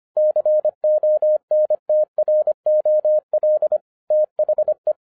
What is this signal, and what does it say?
Morse code, heard as a keyed tone, about 25 wpm, 610 Hz. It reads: CONTROL THE
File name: 2011-10-mp3 / n09-CW TO PS2 CONVERTER 25wpm